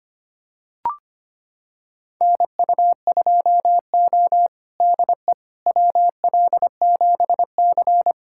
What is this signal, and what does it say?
Morse code, heard as a keyed tone, about 25 wpm, 700 Hz. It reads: NU2O DE WL7C
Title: USA Callsign DE Callsign - 4 Characters - 25wpm